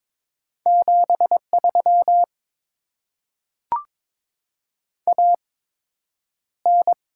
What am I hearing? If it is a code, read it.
Morse code, heard as a keyed tone, about 22 wpm, spaced as for 9 wpm, 700 Hz. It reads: 73 AN
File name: Top 20 QSO Elements - 8x Character Spacing - 22wpm